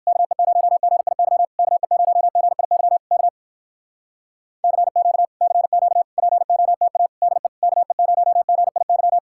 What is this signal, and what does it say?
Morse code, heard as a keyed tone, about 58 wpm, 700 Hz. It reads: KE0GIY KE0GIY K CQ CQ POTA DE KE0GIY